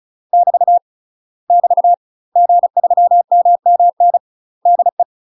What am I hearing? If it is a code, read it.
Morse code, heard as a keyed tone, about 36 wpm, 700 Hz. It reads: <BT> <BT> G3MMN DE
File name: UK MASTER QSOs TRAINING / 35WPM / UK MASTER QSOs 0007